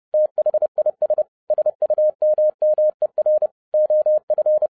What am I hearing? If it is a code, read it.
Morse code, heard as a keyed tone, about 30 wpm, 610 Hz. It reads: THIS SUMMER OF
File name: 2012-01-mp3 / n11-I DIDNT MISS THE SOLAR CYCLE 24-30wpm